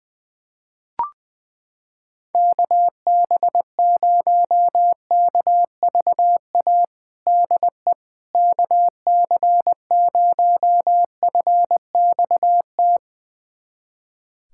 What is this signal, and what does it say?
Morse code, heard as a keyed tone, about 20 wpm, 700 Hz. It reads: KB0KVA DE KC0FXT
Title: USA Callsign DE Callsign - 6 Characters - 20wpm